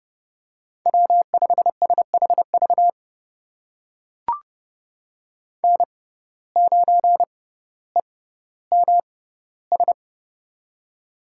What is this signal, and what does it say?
Morse code, heard as a keyed tone, about 30 wpm, spaced as for 15 wpm, 700 Hz. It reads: W5SHV N9EMS
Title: USA Call Signs - 5 Characters - 6x Character Spacing - 30wpm